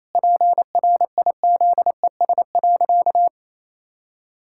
Morse code, heard as a keyed tone, about 28 wpm, 700 Hz. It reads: PRIZES.
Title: Pangram Challenge v2 - 28wpm